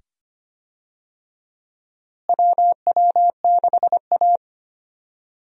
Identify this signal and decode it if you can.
Morse code, heard as a keyed tone, about 25 wpm, 700 Hz. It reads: WW6A
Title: USA Call Signs - 4 Characters - Repeated 2x - 25wpm